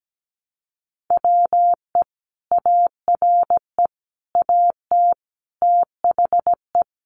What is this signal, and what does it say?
Morse code, heard as a keyed tone, about 17 wpm, 700 Hz. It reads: WE ARE AT THE
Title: Sentences from Top 1000 Words 17wpm